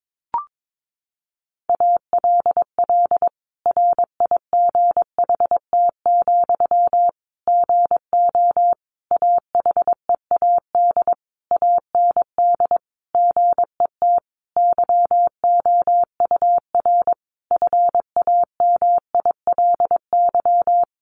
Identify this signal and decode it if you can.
Morse code, heard as a keyed tone, about 22 wpm, 700 Hz. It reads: ALL RIGHT, GO AHEAD AND GET YOUR FAMILY
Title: Sentences from Top 1100 Words 22wpm